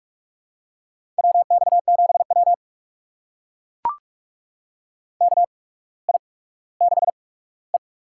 Morse code, heard as a keyed tone, about 44 wpm, spaced as for 19 wpm, 700 Hz. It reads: WX7W KI6E
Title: USA Call Signs - 4 Characters - 8x Character Spacing - 45wpm